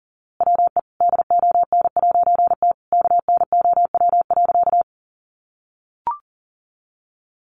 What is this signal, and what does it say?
Morse code, heard as a keyed tone, about 40 wpm, 700 Hz. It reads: WE DON'T KNOW.
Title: A Sound Of Thunder 40wpm